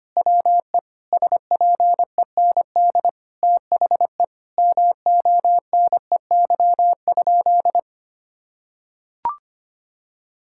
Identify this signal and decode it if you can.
Morse code, heard as a keyed tone, about 25 wpm, 700 Hz. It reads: WE SPEND THE MONEY?
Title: Sentences from Top 1000 Words - Review Part 5 - 25wpm